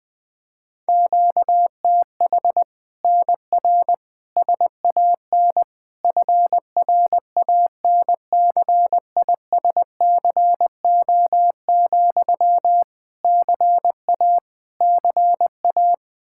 Morse code, heard as a keyed tone, about 20 wpm, 700 Hz. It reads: QTH NR SAN FRANCISCO, CA CA
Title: Practice QSO - Exchange 1 - Beginning - 20wpm